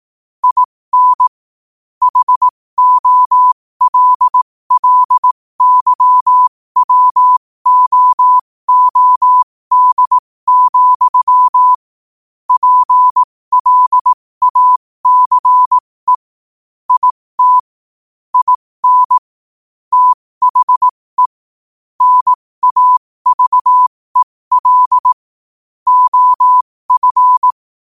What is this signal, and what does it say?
Morse code, heard as a keyed tone, about 18 wpm, spaced as for 15 wpm, 1000 Hz. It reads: IN HOLLYWOOD, PLACE IT IN THE NAVEL OF